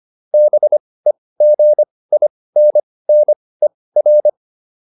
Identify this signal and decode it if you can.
Morse code, heard as a keyed tone, about 25 wpm, spaced as for 18 wpm, 600 Hz. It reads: BEGINNER